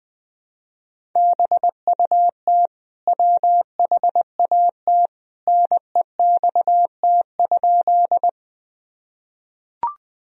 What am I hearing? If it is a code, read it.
Morse code, heard as a keyed tone, about 20 wpm, 700 Hz. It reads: BUT WHAT NEXT?
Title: Sentences from Top 1000 Words - Review Part 1 - 20wpm